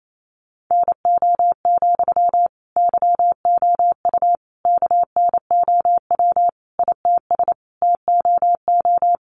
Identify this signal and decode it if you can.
Morse code, heard as a keyed tone, about 28 wpm, 700 Hz. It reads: NO, YOU KNOW ITS TOO
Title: Sentences from Top 800 Words 28wpm